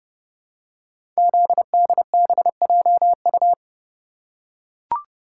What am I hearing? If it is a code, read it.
Morse code, heard as a keyed tone, about 30 wpm, 700 Hz. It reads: ZDBJU
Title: Code Groups - 5 Letters - 30wpm